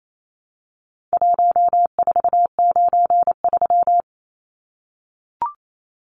Morse code, heard as a keyed tone, about 28 wpm, 700 Hz. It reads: 1493
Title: Numbers - Contest Style - 4-Digit 28wpm